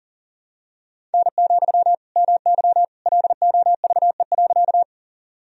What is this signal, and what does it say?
Morse code, heard as a keyed tone, about 40 wpm, 700 Hz. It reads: N, MY LOVE.
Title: Sentences from Top 400 Words and Abbreviations 40wpm